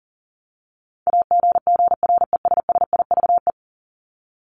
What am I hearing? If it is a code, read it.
Morse code, heard as a keyed tone, about 37 wpm, 700 Hz. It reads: AGGRESSIVE
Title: Top 1501-1750 Words 40wpm